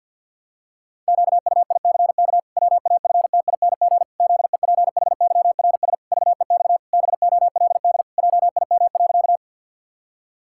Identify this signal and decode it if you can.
Morse code, heard as a keyed tone, about 50 wpm, 700 Hz. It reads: QUICK WAFTING ZEPHYRS VEX BOLD JIM.
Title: Pangram Challenge v2 - 50wpm